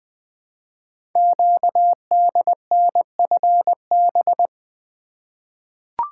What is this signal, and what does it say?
Morse code, heard as a keyed tone, about 20 wpm, 700 Hz. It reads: QDNFB
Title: Code Groups - 5 Letters - 20wpm